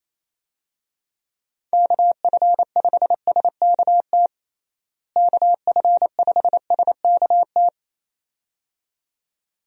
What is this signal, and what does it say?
Morse code, heard as a keyed tone, about 28 wpm, 700 Hz. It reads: KF5SKT KF5SKT
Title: USA Call Signs - 6 Characters - Repeated 2x - 28wpm